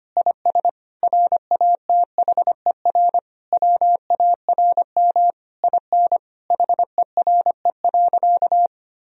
Morse code, heard as a keyed tone, about 25 wpm, 700 Hz. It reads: IS RATHER WARM IN HERE.
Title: Sentences from Top 1000 Words - Review Part 4 - 25wpm